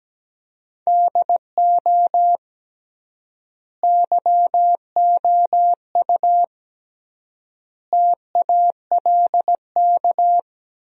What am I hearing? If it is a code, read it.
Morse code, heard as a keyed tone, about 17 wpm, 700 Hz. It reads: DO YOU TALK
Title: 3x Word-Spacing - Sentences from Top 500 Words - 17wpm